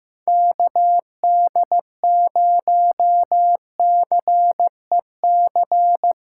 Morse code, heard as a keyed tone, about 15 wpm, 700 Hz. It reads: KD0CEC
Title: USA Callsign DE Callsign - 6 Characters - 15wpm